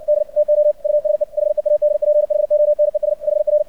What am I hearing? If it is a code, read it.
Morse code, heard as a keyed tone, about 56 wpm, 600 Hz. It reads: D TO USE HEADPHONES HR
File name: archived / CRSnet